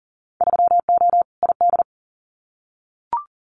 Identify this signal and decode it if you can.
Morse code, heard as a keyed tone, about 42 wpm, 700 Hz. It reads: 3O ID
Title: Winter Field Day Exchanges Encore 40wpm